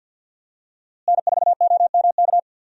Morse code, heard as a keyed tone, about 51 wpm, 700 Hz. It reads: N4OMK
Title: USA Call Signs - 5 Characters - Repeated 3x - 50wpm